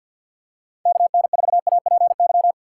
Morse code, heard as a keyed tone, about 48 wpm, 700 Hz. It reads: KN4APY
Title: USA Call Signs - 6 Characters - 3x Character Spacing - 50wpm